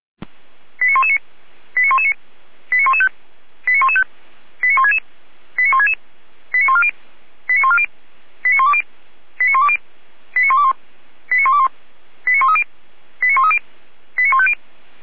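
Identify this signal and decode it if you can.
Five-tone selective call (ZVEI), heard as quick runs of five tones, tones of about 70 ms. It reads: 89119 89119 89115 89115 89166 89166 89140 89140 89120 89120 89121 89121 89130 89130 89150
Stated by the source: ZVEI five-tone call received by SDR, from the feuerwehr-orlamuende.org fire brigade audio archive.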